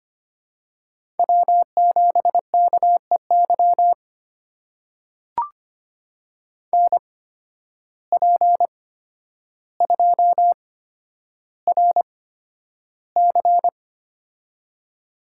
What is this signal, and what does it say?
Morse code, heard as a keyed tone, about 25 wpm, spaced as for 11 wpm, 700 Hz. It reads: W7KEY NP2RC
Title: USA Call Signs - 5 Characters - 8x Character Spacing - 25wpm